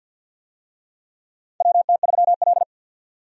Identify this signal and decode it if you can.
Morse code, heard as a keyed tone, about 53 wpm, 700 Hz. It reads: WT3L
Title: USA Call Signs - 4 Characters - 50wpm